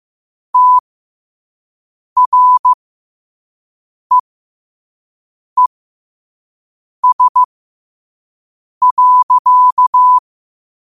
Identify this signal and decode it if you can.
Morse code, heard as a keyed tone, about 15 wpm, spaced as for 5 wpm, 1000 Hz. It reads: TREES.
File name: qotdfiles / 5wpm